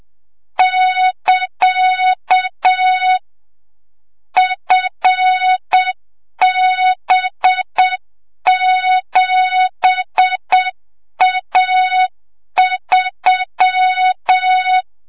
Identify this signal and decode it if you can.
Morse code, heard as a keyed tone, about 7 wpm, 3700 Hz, with noise. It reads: <KA> FB7A3
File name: Morze-HA2MN / hang / 25 GYAKORLAT